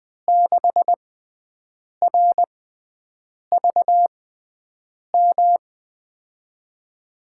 Morse code, heard as a keyed tone, about 20 wpm, spaced as for 7 wpm, 700 Hz. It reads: 6RVM
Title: USA Call Signs - 5 Characters - 6x Character Spacing - 20wpm